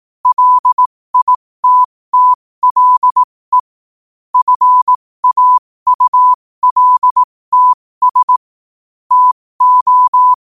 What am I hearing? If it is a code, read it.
Morse code, heard as a keyed tone, about 18 wpm, spaced as for 15 wpm, 1000 Hz. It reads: LITTLE FAULTS TO
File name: qotdfiles / 15wpm